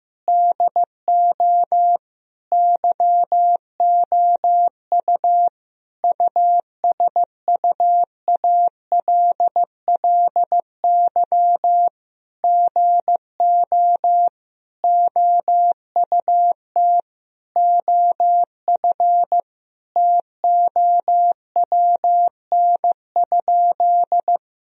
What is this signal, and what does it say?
Morse code, heard as a keyed tone, about 14 wpm, 700 Hz. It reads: DO YOU USUALLY GO OUT OF TOWN?
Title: Sentences from Top 1000 Words - Review Part 3 - 15wpm